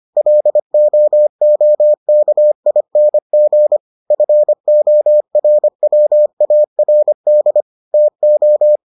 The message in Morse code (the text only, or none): LOOKING FORWARD TO